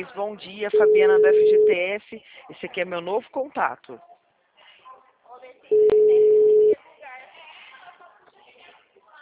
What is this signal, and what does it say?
Ringback tone (Europe, 425 Hz), with speech.